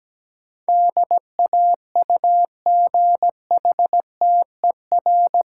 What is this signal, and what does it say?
Morse code, heard as a keyed tone, about 17 wpm, 700 Hz. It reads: DAUGHTER
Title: Sentences from Top 1500 Words - New Word Drill Alt - 17wpm